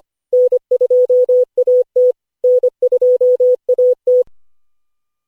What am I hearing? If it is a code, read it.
Morse code, heard as a keyed tone, about 25 wpm, 500 Hz. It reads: N2AT N2AT
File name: cw / Session5 cw training